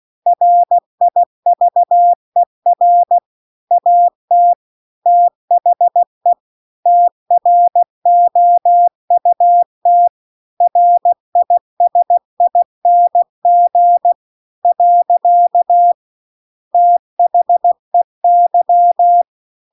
Morse code, heard as a keyed tone, about 16 wpm, 700 Hz. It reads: RIVER AT THE TROUT RISING . THEY